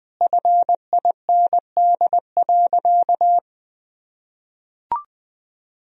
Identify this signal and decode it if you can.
Morse code, heard as a keyed tone, about 21 wpm, 700 Hz. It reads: FIND.
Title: Sentences from Top 1000 Words - Review Part 2 - 20wpm